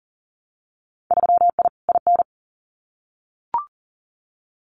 Morse code, heard as a keyed tone, about 40 wpm, 700 Hz. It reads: 3I IN
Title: Winter Field Day Exchanges 40wpm